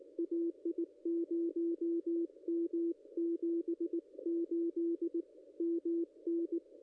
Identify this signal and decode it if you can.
Morse code, heard as a keyed tone, about 19 wpm, 340 Hz, with radio noise. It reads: AI0M 78 MN
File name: contests / ss-cw-k5tr-op-n5rz -0100